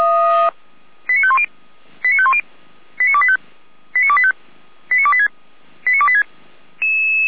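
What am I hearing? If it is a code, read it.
Five-tone selective call (ZVEI), heard as quick runs of five tones, tones of about 70 ms. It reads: 89510 89510 89285 89285 89286 89286
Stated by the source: ZVEI five-tone call received on SDR by Feuerwehr Orlamünde.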